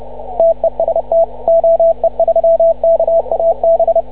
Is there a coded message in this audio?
Yes: Morse code from 0.4 to 4.0 s, heard as a keyed tone, about 29 wpm, 660 Hz, with radio noise. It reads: TEST OE3KAB